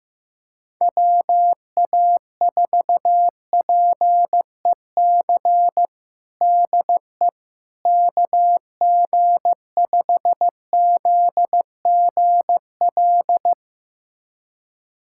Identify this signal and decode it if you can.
Morse code, heard as a keyed tone, about 16 wpm, 700 Hz. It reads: WA4PEC DE KG5ZGL
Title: USA Callsign DE Callsign - 6 Characters - 15wpm